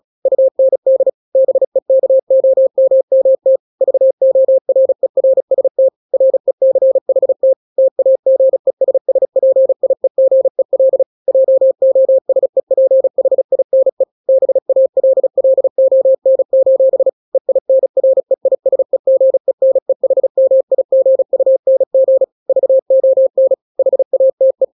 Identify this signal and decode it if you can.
Morse code, heard as a keyed tone, about 36 wpm, 530 Hz. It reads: UND BEKOMMT VORERST RECHT TAGESSPIEGEL JOSEPHINE BALLON: EINREISEGENEHMIGUNG VON HATE